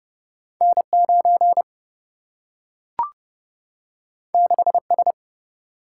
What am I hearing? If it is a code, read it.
Morse code, heard as a keyed tone, about 30 wpm, 700 Hz. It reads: N9 6S
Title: Single-Letter Single-Number 30wpm